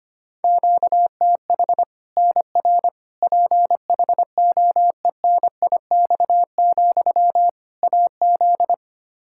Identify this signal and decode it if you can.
Morse code, heard as a keyed tone, about 25 wpm, 700 Hz. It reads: QTH NR PHOENIX, AZ
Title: Practice QSO - Exchange 1 - Beginning - 25wpm